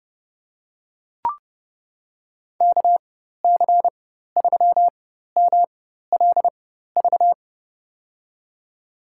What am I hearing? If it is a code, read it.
Morse code, heard as a keyed tone, about 30 wpm, spaced as for 14 wpm, 700 Hz. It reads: KC3MLV
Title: USA Call Signs - 4x Character Spacing - 30wpm